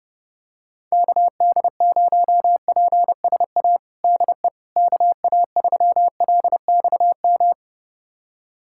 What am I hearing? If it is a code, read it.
Morse code, heard as a keyed tone, about 30 wpm, 700 Hz. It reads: KD0PSA DE KA3LXM
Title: USA Callsign DE Callsign - 6 Characters - 30wpm